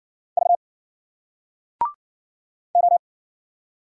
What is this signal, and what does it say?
Morse code, heard as a keyed tone, about 57 wpm, 700 Hz. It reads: V O